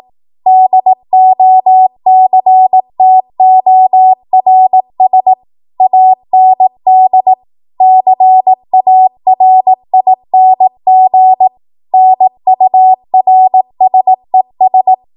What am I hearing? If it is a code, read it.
Morse code, heard as a keyed tone, about 18 wpm, 750 Hz. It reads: DOCTORS AND CARING NURSES
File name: Morse / 18WPM